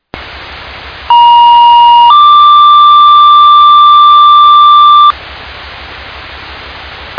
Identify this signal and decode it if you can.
Two-tone page, heard as a one-second tone, then a three-second tone, about 950 then 1180 Hz. It is heard from 1.1 to 5.1 s.